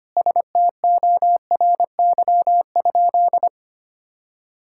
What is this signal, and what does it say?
Morse code, heard as a keyed tone, about 25 wpm, 700 Hz. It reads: STORY?
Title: Sentences from Top 1000 Words - Review Part 5 - 25wpm